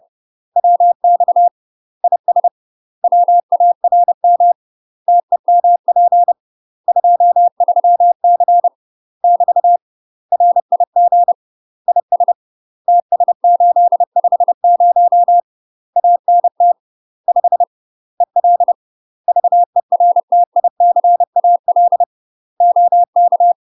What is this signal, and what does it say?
Morse code, heard as a keyed tone, about 30 wpm, 700 Hz. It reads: WX IS WARM TEMP 23C <BT> RIG IS TS850 ANT 5 EL VERTICAL OK
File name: UK MASTER QSOs TRAINING / 30WPM / UK MASTER QSOs 0031